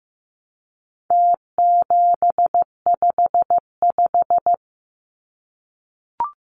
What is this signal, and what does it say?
Morse code, heard as a keyed tone, about 15 wpm, 700 Hz. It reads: T755
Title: Numbers - Contest Style - 4-Digit 15wpm